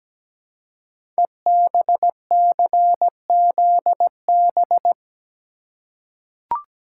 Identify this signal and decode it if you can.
Morse code, heard as a keyed tone, about 17 wpm, 700 Hz. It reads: EBCZB
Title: Code Groups - 5 Letters - 17wpm